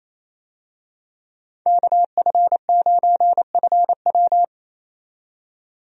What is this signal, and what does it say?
Morse code, heard as a keyed tone, about 28 wpm, 700 Hz. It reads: KF9FW
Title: USA Call Signs - 5 Characters - 28wpm